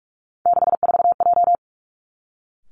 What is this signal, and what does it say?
Morse code, heard as a keyed tone, about 46 wpm, 700 Hz. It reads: 64J
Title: Three Challenging Letter-Number Combinations 45wpm